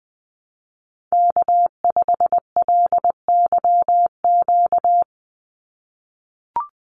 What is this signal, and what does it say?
Morse code, heard as a keyed tone, about 20 wpm, 700 Hz. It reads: K5LYQ
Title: USA Call Signs 20wpm